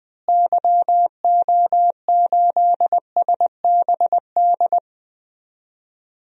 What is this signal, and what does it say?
Morse code, heard as a keyed tone, about 20 wpm, 700 Hz. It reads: YO8SBD
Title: DXCC-100 Call Signs - 6 Characters - Repeated 3x - 20wpm